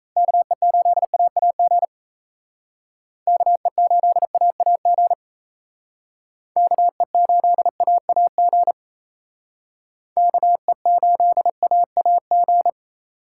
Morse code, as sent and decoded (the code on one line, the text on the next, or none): -.- . ---.. .- .- --. / -.- . ---.. .- .- --. / -.- . ---.. .- .- --. / -.- . ---.. .- .- --.
KE8AAG KE8AAG KE8AAG KE8AAG